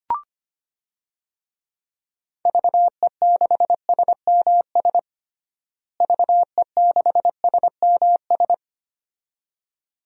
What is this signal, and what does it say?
Morse code, heard as a keyed tone, about 25 wpm, 700 Hz. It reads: VE6SMS VE6SMS
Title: DXCC-100 Call Signs - 6 Characters - Repeated 2x - 25wpm